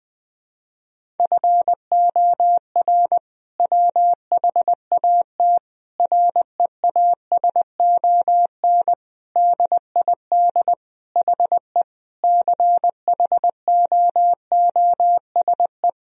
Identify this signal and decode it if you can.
Morse code, heard as a keyed tone, about 20 wpm, 700 Hz. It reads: FOR WHAT REASON DID HE CHOOSE
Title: Sentences from Top 1000 Words - Review Part 4 - 20wpm